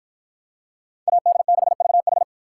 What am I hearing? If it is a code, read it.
Morse code, heard as a keyed tone, about 53 wpm, 700 Hz. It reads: AD6VH